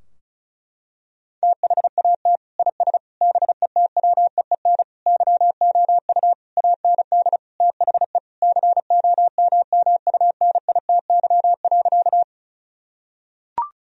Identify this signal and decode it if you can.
Morse code, heard as a keyed tone, about 36 wpm, 700 Hz. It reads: THAT IS BETWEEN YOU AND THE COMMUNITY.
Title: Sentences from Top 1000 Words - Review Part 4 - 35wpm